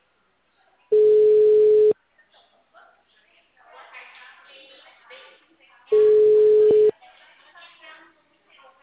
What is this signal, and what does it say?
Ringback tone (Europe, 425 Hz), with speech.
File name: out- -2035- - - .6137